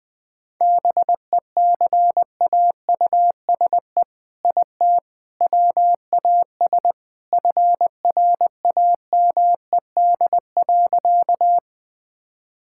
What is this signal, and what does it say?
Morse code, heard as a keyed tone, about 20 wpm, 700 Hz. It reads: BECAUSE IT WAS FRAMED.
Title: All-Age Appropriate Jokes - 20wpm